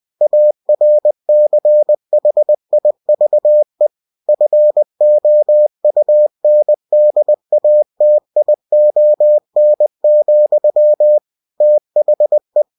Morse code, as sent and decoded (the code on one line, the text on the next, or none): .- .-. -.-. .... .. ...- . / ..-. --- ..- -. -.. .- - .. --- -. --..-- / - .... .
ARCHIVE FOUNDATION, THE